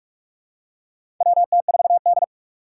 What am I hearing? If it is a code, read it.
Morse code, heard as a keyed tone, about 44 wpm, 700 Hz. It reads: WT4D
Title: USA Call Signs - 4 Characters - 6x Character Spacing - 45wpm